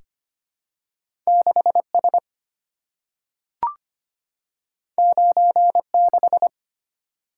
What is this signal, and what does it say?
Morse code, heard as a keyed tone, about 25 wpm, 700 Hz. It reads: 6S 96
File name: Two Letter-Number Combinations - 25wpm